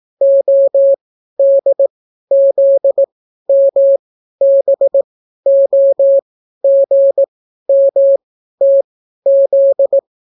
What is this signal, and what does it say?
Morse code, heard as a keyed tone, about 18 wpm, spaced as for 12 wpm, 550 Hz. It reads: ODZMBOGMTZ